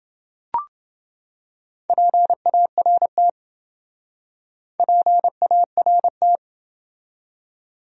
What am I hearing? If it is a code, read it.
Morse code, heard as a keyed tone, about 28 wpm, 700 Hz. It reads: PART PART